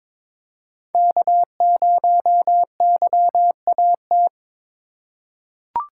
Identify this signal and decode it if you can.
Morse code, heard as a keyed tone, about 21 wpm, 700 Hz. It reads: K0YAT
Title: USA Call Signs - 5 Characters - 3x Character Spacing - 22wpm